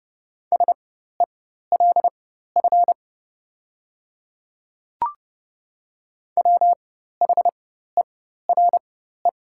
Morse code, heard as a keyed tone, about 30 wpm, spaced as for 14 wpm, 700 Hz. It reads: SELF WHERE
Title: 4x Character Spacing - Top 300 Words - 30wpm